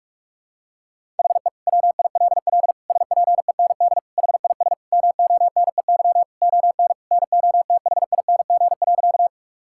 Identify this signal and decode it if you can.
Morse code, heard as a keyed tone, about 46 wpm, 700 Hz. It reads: HE WILL SPEND HIS MONEY ON NOTHING.
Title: Sentences from Top 1000 Words - Review Part 5 - 45wpm